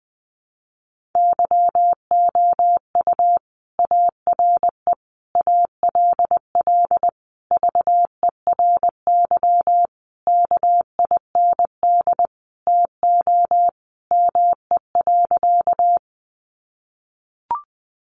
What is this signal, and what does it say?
Morse code, heard as a keyed tone, about 20 wpm, 700 Hz. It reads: YOU ARE ALL VERY KIND TO ME.
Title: Sentences from Top 200 Words 20wpm